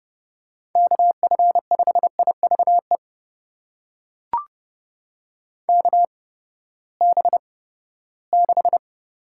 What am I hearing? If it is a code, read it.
Morse code, heard as a keyed tone, about 30 wpm, spaced as for 17 wpm, 700 Hz. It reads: KF5IVE KB6
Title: USA Call Signs - 6 Characters - 8x Character Spacing - 30wpm